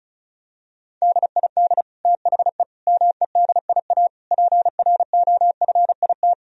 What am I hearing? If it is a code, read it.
Morse code, heard as a keyed tone, about 36 wpm, 700 Hz. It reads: DID THE MEDIA PROFIT
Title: Sentences from Top 1000 Words - Review Part 4 - 35wpm